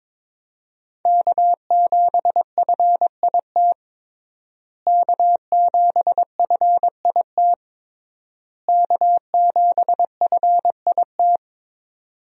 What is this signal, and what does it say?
Morse code, heard as a keyed tone, about 22 wpm, 700 Hz. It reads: K7FIT K7FIT K7FIT
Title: USA Call Signs - 5 Characters - Repeated 3x - 22wpm